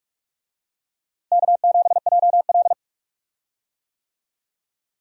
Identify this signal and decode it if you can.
Morse code, heard as a keyed tone, about 44 wpm, 700 Hz. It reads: K7JL
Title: USA Call Signs - 4 Characters - Repeated 2x - 45wpm